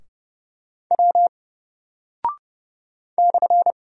Morse code, heard as a keyed tone, about 31 wpm, 700 Hz. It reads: W /
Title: Single Character (Letter:Number:Punctuation:Prosign) 30wpm